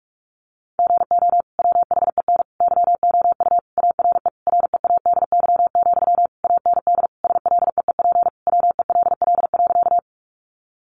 Morse code, heard as a keyed tone, about 44 wpm, 700 Hz. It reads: GO WHEN YOU ARE READY, AND SLEEP WELL.
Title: Sentences from Top 500 Words 45wpm